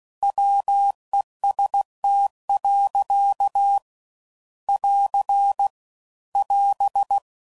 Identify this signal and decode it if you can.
Morse code, heard as a keyed tone, about 16 wpm, 800 Hz. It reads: WEST. <AR> <AS>
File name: CW Code Groups @15WPM / Weather Sets / Weather 5 1